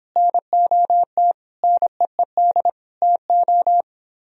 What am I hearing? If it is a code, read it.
Morse code, heard as a keyed tone, about 25 wpm, 700 Hz. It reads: NOT NEED TO